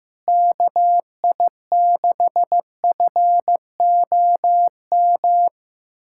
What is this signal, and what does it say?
Morse code, heard as a keyed tone, about 15 wpm, 700 Hz. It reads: KI6FOM
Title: USA Callsign DE Callsign - 6 Characters - 15wpm